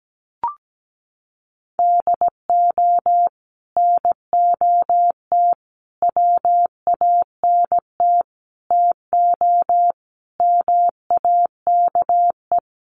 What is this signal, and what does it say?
Morse code, heard as a keyed tone, about 17 wpm, 700 Hz. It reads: DO NOT WANT TO MAKE
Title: Sets of 5 Words Encore 17wpm